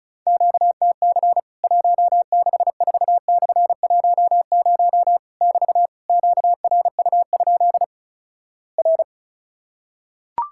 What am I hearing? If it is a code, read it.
Morse code, heard as a keyed tone, about 36 wpm, 700 Hz. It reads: QTC 164/10 <BT> QRU? R
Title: Worked All Europe DX Contest - QTC Exchange - 35wpm